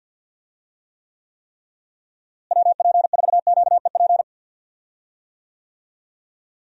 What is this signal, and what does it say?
Morse code, heard as a keyed tone, about 46 wpm, 700 Hz. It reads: WP4QEP
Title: USA Call Signs - 6 Characters - Repeated 3x - 50wpm